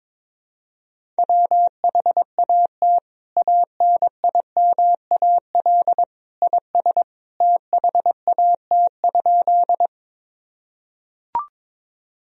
Morse code, heard as a keyed tone, about 22 wpm, 700 Hz. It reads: WHAT ANIMAL IS THAT?
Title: Sentences from Top 1000 Words - Review Part 1 - 22wpm